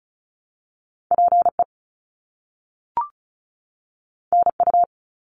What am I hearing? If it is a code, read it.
Morse code, heard as a keyed tone, about 36 wpm, 700 Hz. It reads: PE NU
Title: Canadian Provinces and Territories 35wpm